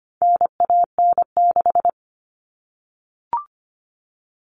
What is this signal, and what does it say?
Morse code, heard as a keyed tone, about 25 wpm, 700 Hz. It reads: NAN6
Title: All Cut Numbers - 4-Digit 25wpm